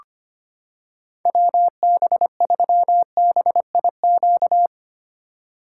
Morse code, heard as a keyed tone, about 25 wpm, 700 Hz. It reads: WB3BIQ
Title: K1USN SST QSO Practice - 25wpm